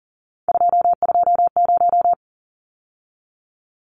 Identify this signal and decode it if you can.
Morse code, heard as a keyed tone, about 40 wpm, 700 Hz. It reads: 220
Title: Numbers - 3-Digit 40wpm